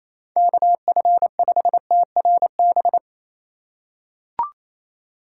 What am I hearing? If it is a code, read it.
Morse code, heard as a keyed tone, about 28 wpm, 700 Hz. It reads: KF5TRB
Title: USA Call Signs - 6 Characters - 2x Character Spacing - 28wpm